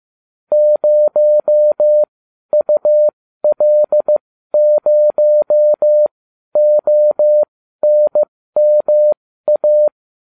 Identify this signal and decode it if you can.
Morse code, heard as a keyed tone, about 15 wpm, spaced as for 12 wpm, 600 Hz. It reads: 0UL0ONMA
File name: cw / ubacw20 3